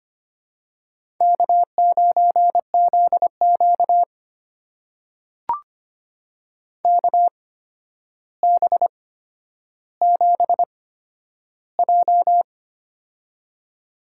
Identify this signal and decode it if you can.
Morse code, heard as a keyed tone, about 25 wpm, spaced as for 11 wpm, 700 Hz. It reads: K9ZQ KB7J
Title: USA Call Signs - 8x Character Spacing - 25wpm